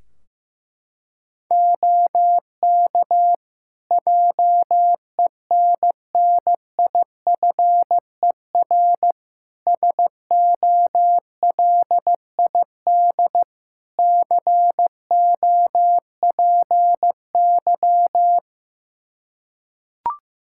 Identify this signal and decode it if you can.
Morse code, heard as a keyed tone, about 16 wpm, 700 Hz. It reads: OK JENNIFER SOLID COPY
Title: Practice QSO - Exchange 3 - More Info - 15wpm